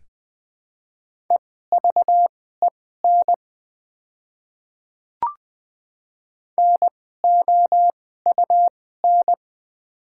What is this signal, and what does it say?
Morse code, heard as a keyed tone, about 20 wpm, spaced as for 15 wpm, 700 Hz. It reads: EVEN NOUN
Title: 2x Character Spacing - Top 500 Words - 20wpm